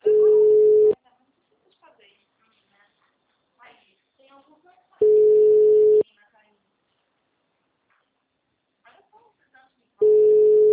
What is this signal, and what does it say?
Ringback tone (Europe, 425 Hz), with speech.